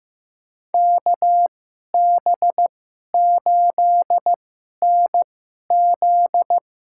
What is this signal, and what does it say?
Morse code, heard as a keyed tone, about 15 wpm, spaced as for 11 wpm, 700 Hz. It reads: KB8NZ
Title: USA Call Signs - 2x Character Spacing - 15wpm